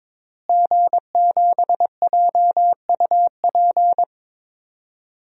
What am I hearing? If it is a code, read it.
Morse code, heard as a keyed tone, about 22 wpm, 700 Hz. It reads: G7JUP
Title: DXCC-100 Call Signs - 5 Characters - Repeated 2x - 22wpm